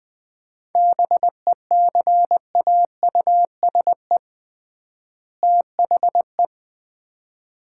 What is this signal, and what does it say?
Morse code, heard as a keyed tone, about 20 wpm, 700 Hz. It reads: BECAUSE THE
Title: 3x Word-Spacing - Sets of 2 Words - 20wpm